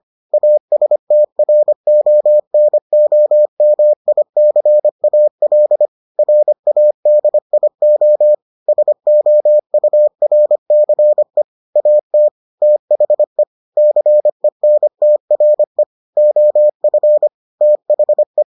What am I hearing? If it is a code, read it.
Morse code, heard as a keyed tone, about 25 wpm, 600 Hz. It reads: ASTRONOMICAL RADIO SOURCE AT THE CENTRE OF THE